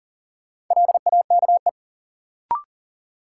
Morse code, heard as a keyed tone, about 40 wpm, 700 Hz. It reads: LAKE
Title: Sentences from Top 1100 Words - New Word Drill Alt - 40wpm